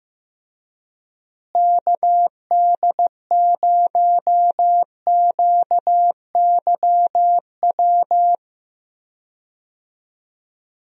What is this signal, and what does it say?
Morse code, heard as a keyed tone, about 15 wpm, 700 Hz. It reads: KD0QYW
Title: USA Call Signs - 6 Characters - Repeated 2x - 15wpm